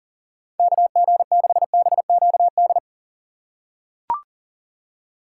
Morse code, heard as a keyed tone, about 38 wpm, 700 Hz. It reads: KG6BQD
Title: USA Call Signs - 6 Characters - 3x Character Spacing - 40wpm